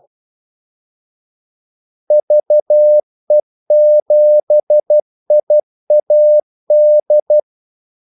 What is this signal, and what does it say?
Morse code, heard as a keyed tone, about 12 wpm, 600 Hz. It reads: VE7IAD
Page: 2 ekstra ordmellemrum mellem kaldesignalerne